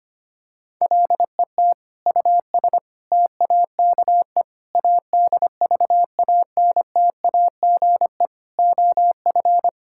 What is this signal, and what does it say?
Morse code, heard as a keyed tone, about 25 wpm, 700 Hz. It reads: LET US TAKE ADVANTAGE OF
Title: Sentences from Top 1000 Words - Review Part 6 - 25wpm